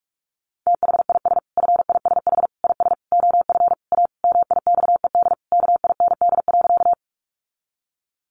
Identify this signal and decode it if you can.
Morse code, heard as a keyed tone, about 44 wpm, 700 Hz. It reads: THIS FISH IS OF A MIXED KIND.
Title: Sentences from Top 1500 Words 45wpm